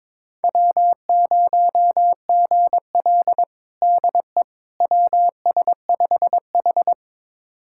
Morse code, heard as a keyed tone, about 22 wpm, 700 Hz. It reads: W0GL DE WS5H
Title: USA Callsign DE Callsign - 4 Characters - 22wpm